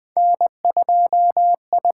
Morse code, heard as a keyed tone, about 20 wpm, 700 Hz. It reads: N2I
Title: USA Call Signs - 6x Character Spacing - 20wpm